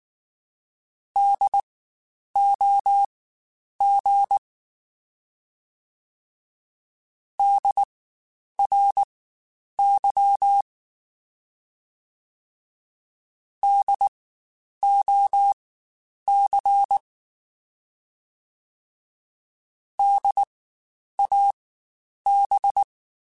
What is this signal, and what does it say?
Morse code, heard as a keyed tone, about 19 wpm, spaced as for 9 wpm, 800 Hz. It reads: DOG DRY DOC DAB
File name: Spaced Alphas / Spaced Alphas mp3s / Ds sp @18